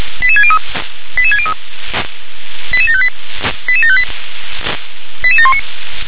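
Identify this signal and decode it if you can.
Five-tone selective call (ZVEI), heard as quick runs of five tones, tones of about 70 ms. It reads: 88603 88603 88758 88758 88719
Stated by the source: ZVEI five-tone selective call received on SDR.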